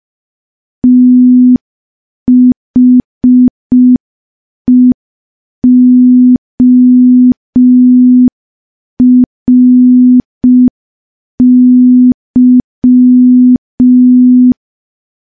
Morse code, heard as a keyed tone, about 5 wpm, 250 Hz. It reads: THEORY
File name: morse / morse-training-05WPM